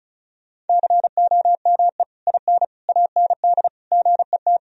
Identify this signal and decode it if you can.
Morse code, heard as a keyed tone, about 34 wpm, 700 Hz. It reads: COME IN AND GET
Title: Sentences from Top 1000 Words - Review Part 2 - 35wpm